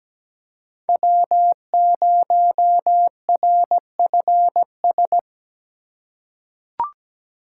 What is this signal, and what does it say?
Morse code, heard as a keyed tone, about 17 wpm, 700 Hz. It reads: W0RFS
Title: USA Call Signs - 3x Character Spacing - 17wpm